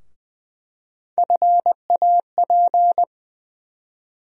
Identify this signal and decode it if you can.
Morse code, heard as a keyed tone, about 21 wpm, 700 Hz. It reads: FAP
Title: USA Call Sign Postfixes - 3 Letters 20wpm